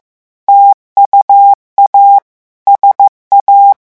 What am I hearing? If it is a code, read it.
Morse code, heard as a keyed tone, about 15 wpm, 800 Hz. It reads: TUA SA
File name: morse mp3 / revision9 citations humoristiques vitesse 14 mots minute